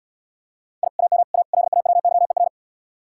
Morse code, heard as a keyed tone, about 77 wpm, 700 Hz. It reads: I AM A LAWYER